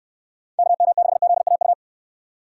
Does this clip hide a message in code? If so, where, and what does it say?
Yes: Morse code from 0.6 to 1.7 s, heard as a keyed tone, about 69 wpm, 700 Hz. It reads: KM6ZAU